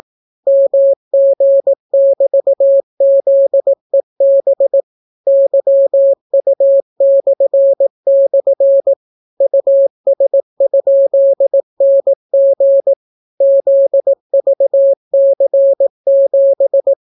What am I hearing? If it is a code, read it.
Morse code, heard as a keyed tone, about 18 wpm, 550 Hz. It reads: MG<BT>ZEB YU// US?NG ZVC7